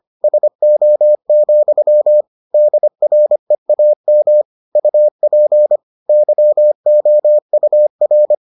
Morse code, heard as a keyed tone, about 25 wpm, 600 Hz. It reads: SO, DREAM UP YOUR